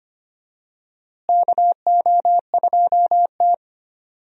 Morse code, heard as a keyed tone, about 25 wpm, 700 Hz. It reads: KO2T
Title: USA Call Signs - 4 Characters - Repeated 3x - 25wpm